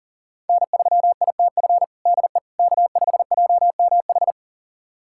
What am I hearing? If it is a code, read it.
Morse code, heard as a keyed tone, about 40 wpm, 700 Hz. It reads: N3ITF DE K5JMH
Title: USA Callsign DE Callsign - 5 Characters - 40wpm